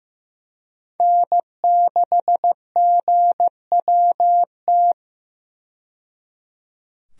Morse code, heard as a keyed tone, about 15 wpm, 700 Hz. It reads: N6GWT
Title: USA Call Signs - 5 Characters - 15wpm